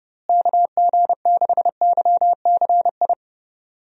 Morse code, heard as a keyed tone, about 32 wpm, 700 Hz. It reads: KG6YCI
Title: USA Call Signs - 6 Characters - Repeated 3x - 30wpm